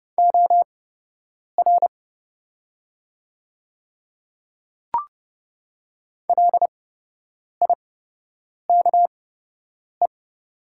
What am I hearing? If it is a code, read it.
Morse code, heard as a keyed tone, about 30 wpm, spaced as for 8 wpm, 700 Hz. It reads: OR LIKE
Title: 8x Character Spacing - Top 100 Words - 30wpm